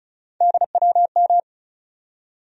Morse code, heard as a keyed tone, about 34 wpm, 700 Hz. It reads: DWM
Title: USA Call Sign Postfixes - 3 Letters 35wpm